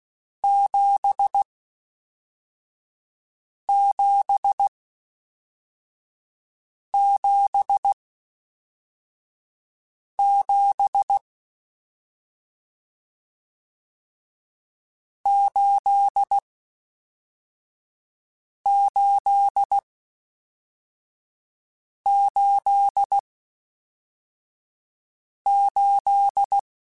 Morse code, heard as a keyed tone, about 16 wpm, spaced as for 4 wpm, 800 Hz. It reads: 7777 8888